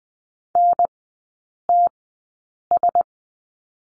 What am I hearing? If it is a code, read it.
Morse code, heard as a keyed tone, about 20 wpm, spaced as for 8 wpm, 700 Hz. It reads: NTS
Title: 053 Lesson 10e - Letter r - Three Characters 20wpm